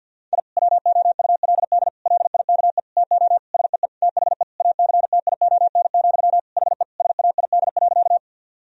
Morse code, heard as a keyed tone, about 46 wpm, 700 Hz. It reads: I WOULD LIKE TO SEE THE ACTION, HE SAID.
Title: Sentences from Top 1000 Words - Review Part 4 - 50wpm